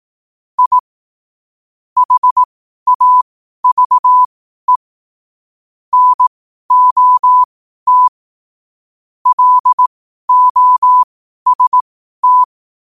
Morse code, heard as a keyed tone, about 18 wpm, spaced as for 12 wpm, 1000 Hz. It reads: I HAVE NOT LOST
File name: qotdfiles / 12wpm